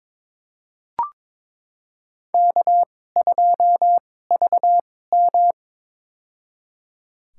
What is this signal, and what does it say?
Morse code, heard as a keyed tone, about 22 wpm, spaced as for 16 wpm, 700 Hz. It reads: K2VM
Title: USA Call Signs - 2x Character Spacing - 22wpm